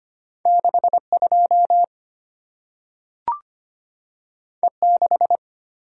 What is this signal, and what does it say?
Morse code, heard as a keyed tone, about 25 wpm, 700 Hz. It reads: 62 E6
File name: Two Letter-Number Combinations - 25wpm